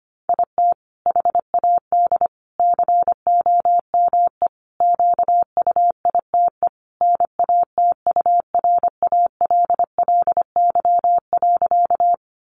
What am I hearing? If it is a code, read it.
Morse code, heard as a keyed tone, about 25 wpm, 700 Hz. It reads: IT HAD COME QUITE NATURALLY.
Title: Sentences from Top 1300 Words 25wpm